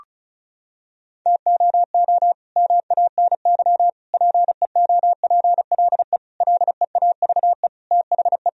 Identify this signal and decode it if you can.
Morse code, heard as a keyed tone, about 34 wpm, 700 Hz. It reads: TOO MANY PEOPLE LEAVE THE
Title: Sentences from Top 1000 Words - Review Part 1 - 35wpm